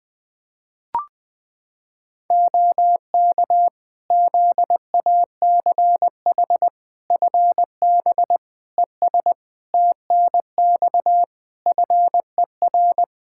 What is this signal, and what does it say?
Morse code, heard as a keyed tone, about 20 wpm, 700 Hz. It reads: OK ZACH FB ES TNX FER
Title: Practice QSO - Exchange 2 - Info - 20wpm